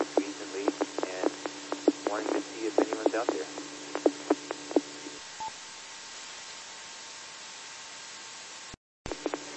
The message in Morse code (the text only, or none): none